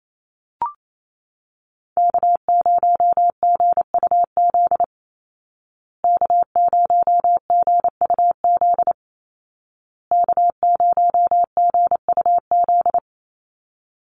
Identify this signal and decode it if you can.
Morse code, heard as a keyed tone, about 28 wpm, 700 Hz. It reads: K0GUZ K0GUZ K0GUZ
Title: Call Signs Repeated 2x - 28wpm